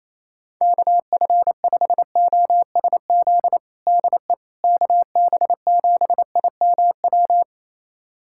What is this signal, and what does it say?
Morse code, heard as a keyed tone, about 28 wpm, 700 Hz. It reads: KF5OSZ DE KB7IMW
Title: USA Callsign DE Callsign 28wpm